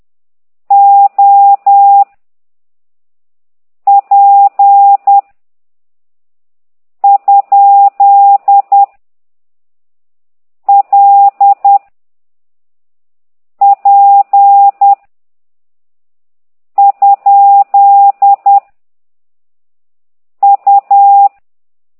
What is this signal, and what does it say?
Morse code, heard as a keyed tone, about 10 wpm, 800 Hz. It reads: O P ? L P ? U